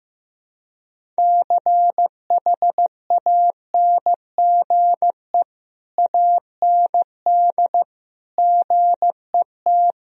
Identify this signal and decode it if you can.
Morse code, heard as a keyed tone, about 16 wpm, 700 Hz. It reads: CHANGE AND GET
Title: Sentences from Top 1000 Words - Review Part 1 - 15wpm